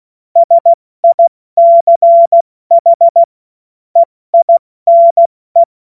Morse code, heard as a keyed tone, about 16 wpm, 680 Hz. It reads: SICH EINE
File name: Morsekurs / Witze / Witz 27. Juli 2021, 70 F80